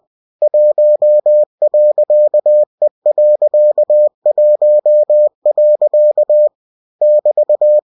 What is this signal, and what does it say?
Morse code, heard as a keyed tone, about 20 wpm, 600 Hz. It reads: 1.E.1. <BT>
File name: cw-books / winnie-the-pooh / winnie-20wpm-page-0116